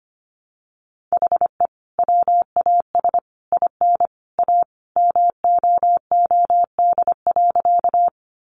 Morse code, heard as a keyed tone, about 25 wpm, 700 Hz. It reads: HE WAS IN A MOOD.
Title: Sentences from Top 1100 Words 25wpm